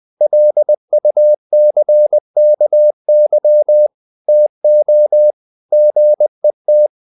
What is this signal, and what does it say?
Morse code, heard as a keyed tone, about 20 wpm, 600 Hz. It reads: LUCKY TO GET
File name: the old man and the sea / the old man and the sea part 1 20wpm